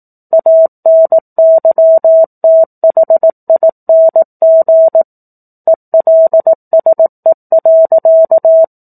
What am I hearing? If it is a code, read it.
Morse code, heard as a keyed tone, about 18 wpm, 650 Hz. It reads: ANYTHING ELSE.
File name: 04-Chapter-04 18 WPM